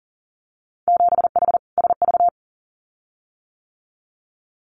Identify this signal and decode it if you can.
Morse code, heard as a keyed tone, about 40 wpm, 700 Hz. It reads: 7H SV
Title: Winter Field Day Exchanges Encore 40wpm